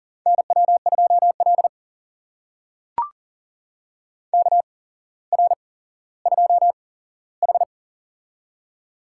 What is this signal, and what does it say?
Morse code, heard as a keyed tone, about 44 wpm, spaced as for 17 wpm, 700 Hz. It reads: NW2L KR2H
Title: USA Call Signs - 4 Characters - 8x Character Spacing - 40wpm